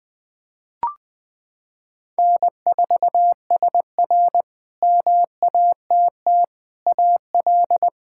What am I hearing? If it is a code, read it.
Morse code, heard as a keyed tone, about 21 wpm, 700 Hz. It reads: N4SR MATT AL
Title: K1USN SST QSO Practice - 20wpm